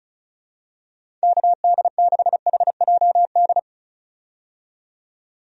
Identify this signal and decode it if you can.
Morse code, heard as a keyed tone, about 34 wpm, 700 Hz. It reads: KD6HJD
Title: USA Call Signs - 6 Characters - 35wpm